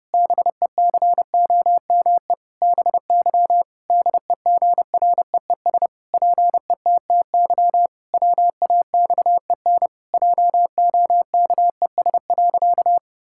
Morse code, heard as a keyed tone, about 30 wpm, 700 Hz. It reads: BECOME BY DEGREES PETTY WAXEN JOKES.
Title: Pangram Challenge v2 - 30wpm